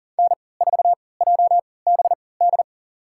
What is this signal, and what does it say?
Morse code, heard as a keyed tone, about 40 wpm, spaced as for 23 wpm, 700 Hz. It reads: N4JBD
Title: USA Call Signs - 3x Character Spacing - 40wpm